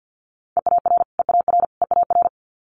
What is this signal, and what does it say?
Morse code, heard as a keyed tone, about 50 wpm, 700 Hz. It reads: EAR EAR EAR
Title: Multi-letter Phonemes Repeated 3x 50wpm